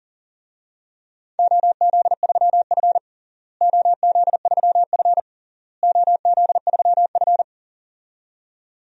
Morse code, heard as a keyed tone, about 37 wpm, 700 Hz. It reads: OZ3F OZ3F OZ3F
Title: DXCC-100 Call Signs - 4 Characters - Repeated 3x - 40wpm